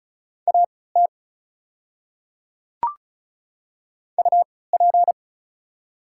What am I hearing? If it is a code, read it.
Morse code, heard as a keyed tone, about 36 wpm, spaced as for 20 wpm, 690 Hz. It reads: AT UP
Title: 3x Character Spacing - Top 100 Words - 35wpm